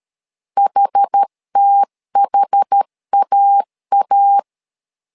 Morse code, heard as a keyed tone, about 13 wpm, 800 Hz. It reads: HTHAA
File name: LEZIONI / 2 UHDN / 2 UHD 12wpm 1540